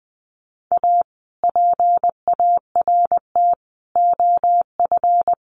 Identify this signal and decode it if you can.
Morse code, heard as a keyed tone, about 20 wpm, 700 Hz. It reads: A PART OF
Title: Sets of 4 Words 20wpm